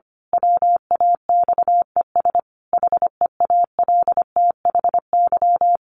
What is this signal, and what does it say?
Morse code, heard as a keyed tone, about 25 wpm, 700 Hz. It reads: WAXES HEALTHY
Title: A Sound Of Thunder 25wpm